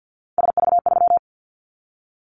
Morse code, heard as a keyed tone, about 50 wpm, 700 Hz. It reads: SV3
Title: Callsign Prefixes for Top 50 DXCC - 50wpm